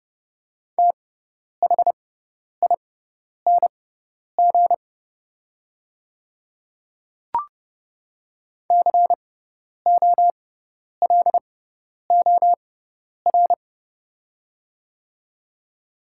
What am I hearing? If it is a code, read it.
Morse code, heard as a keyed tone, about 30 wpm, spaced as for 10 wpm, 700 Hz. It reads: THING COLOR
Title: 6x Character Spacing - Top 300 Words - 30wpm